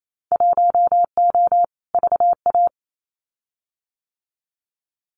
Morse code, heard as a keyed tone, about 28 wpm, 700 Hz. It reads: 1O VA
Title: Winter Field Day Exchanges 28wpm